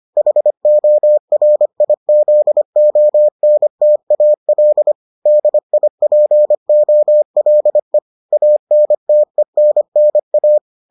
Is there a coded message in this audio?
Yes: Morse code from 0.2 to 10.6 s, heard as a keyed tone, about 25 wpm, 600 Hz. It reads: HORIZONTAL DIPOLE ANTENNA